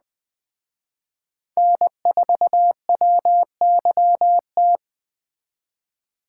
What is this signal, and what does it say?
Morse code, heard as a keyed tone, about 20 wpm, 700 Hz. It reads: N4WYT
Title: Speed Racing - USA 1x3 Call Signs 17wpm